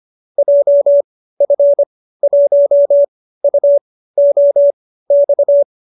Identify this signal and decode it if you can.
Morse code, heard as a keyed tone, about 25 wpm, spaced as for 15 wpm, 570 Hz. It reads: JF1UOX